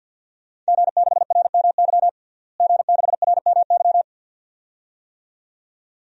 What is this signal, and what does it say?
Morse code, heard as a keyed tone, about 47 wpm, 700 Hz. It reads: G6RMY G6RMY
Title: DXCC-100 Call Signs - 5 Characters - Repeated 2x - 50wpm